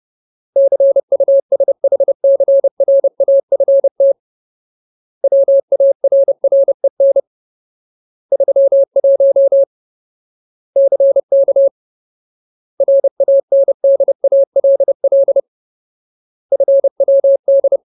Morse code, heard as a keyed tone, about 30 wpm, 550 Hz. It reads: CUSHCRAFT WARREN 31 CK RANDALL FWD
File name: Ultimate Word List / 30WPM Ultimate